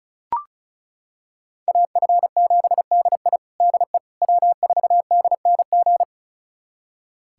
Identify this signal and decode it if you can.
Morse code, heard as a keyed tone, about 34 wpm, 700 Hz. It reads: AF7DI DE W4DNG
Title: USA Callsign DE Callsign - 5 Characters - 35wpm